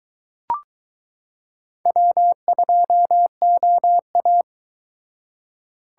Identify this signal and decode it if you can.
Morse code, heard as a keyed tone, about 22 wpm, 700 Hz. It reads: W2OA